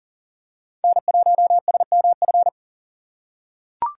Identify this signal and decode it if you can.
Morse code, heard as a keyed tone, about 40 wpm, 690 Hz. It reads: N1SMF
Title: USA Call Signs - 4x Character Spacing - 40wpm